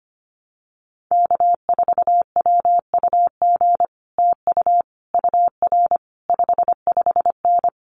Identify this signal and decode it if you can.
Morse code, heard as a keyed tone, about 25 wpm, 700 Hz. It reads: K4WUG TU UR 55N
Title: Common QSO Phrases Encore 25wpm